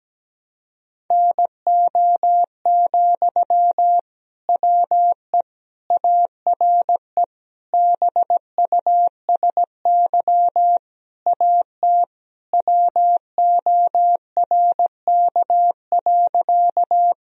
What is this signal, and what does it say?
Morse code, heard as a keyed tone, about 17 wpm, 700 Hz. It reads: NO, WE ARE BUSY AT WORK.
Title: Sentences from Top 1000 Words - Review Part 3 - 17wpm